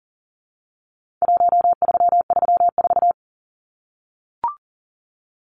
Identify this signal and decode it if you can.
Morse code, heard as a keyed tone, about 40 wpm, 700 Hz. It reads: 1334
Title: Numbers - Contest Style - 4-Digit 40wpm